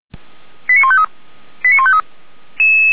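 Five-tone selective call (ZVEI), heard as quick runs of five tones, tones of about 70 ms. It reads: 89153 89153
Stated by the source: ZVEI five-tone selective call received by SDR.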